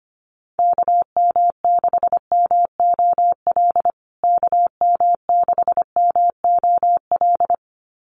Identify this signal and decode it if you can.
Morse code, heard as a keyed tone, about 25 wpm, 700 Hz. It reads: KM6MOL KM6MOL
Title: USA Call Signs Repeated 2x 25wpm